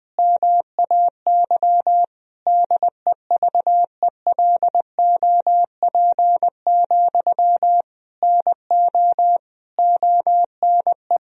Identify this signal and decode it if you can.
Morse code, heard as a keyed tone, about 19 wpm, 700 Hz. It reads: MAY DEVELOP, NO ONE
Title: Sentences from Top 1000 Words - Review Part 2 - 20wpm